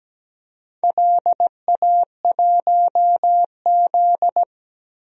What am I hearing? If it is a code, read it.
Morse code, heard as a keyed tone, about 17 wpm, 700 Hz. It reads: LA1Z
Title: DXCC-100 Call Signs - 4 Characters - 17wpm